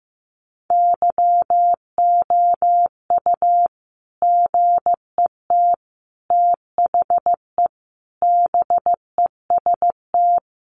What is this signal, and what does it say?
Morse code, heard as a keyed tone, about 15 wpm, 700 Hz. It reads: YOU GET THE BEST
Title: Sentences from Top 1200 Words 15wpm